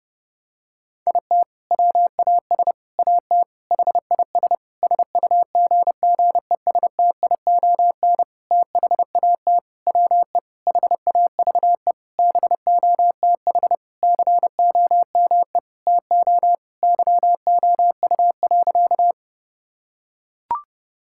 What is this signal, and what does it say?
Morse code, heard as a keyed tone, about 30 wpm, 700 Hz. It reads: IT WAS AT HIS SUGGESTION THAT WE HAVE BOTH COME TO YOU.
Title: Sentences from Top 1600 Words 30wpm